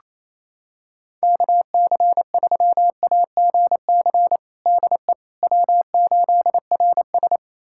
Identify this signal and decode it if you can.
Morse code, heard as a keyed tone, about 28 wpm, 700 Hz. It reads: KC3AGC DE W8RS
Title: USA Callsign DE Callsign 28wpm